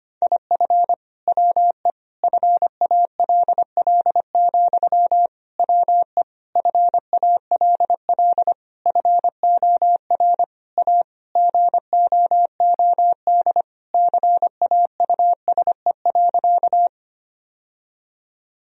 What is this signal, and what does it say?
Morse code, heard as a keyed tone, about 25 wpm, 700 Hz. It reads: IF WE FALL, WE FALL FOR A GOOD CAUSE.
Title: Sentences from Top 1000 Words - Review Part 2 - 25wpm